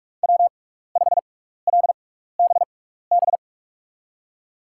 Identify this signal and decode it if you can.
Morse code, heard as a keyed tone, about 44 wpm, spaced as for 15 wpm, 700 Hz. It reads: W5LBB
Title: USA Call Signs - 5 Characters - 6x Character Spacing - 45wpm